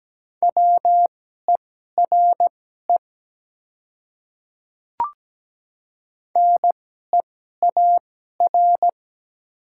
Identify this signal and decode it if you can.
Morse code, heard as a keyed tone, about 17 wpm, spaced as for 12 wpm, 700 Hz. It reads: WERE NEAR
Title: 2x Character Spacing - Top 200 Words - 17wpm